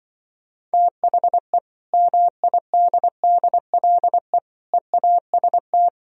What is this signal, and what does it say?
Morse code, heard as a keyed tone, about 24 wpm, 700 Hz. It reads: THE MIDDLE EAST
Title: Speed-Racing - Sets of 3 Words 20wpm